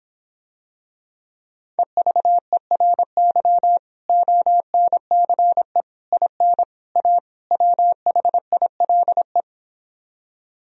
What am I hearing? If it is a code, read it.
Morse code, heard as a keyed tone, about 26 wpm, 700 Hz. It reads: EVERY ONCE IN A WHILE